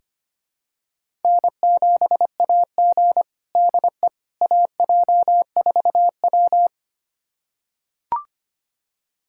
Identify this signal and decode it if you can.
Morse code, heard as a keyed tone, about 25 wpm, 700 Hz. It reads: N7AG DE AJ4W
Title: USA Callsign DE Callsign - 4 Characters - 25wpm